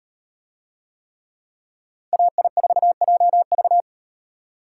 Morse code, heard as a keyed tone, about 34 wpm, 700 Hz. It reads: AI4JV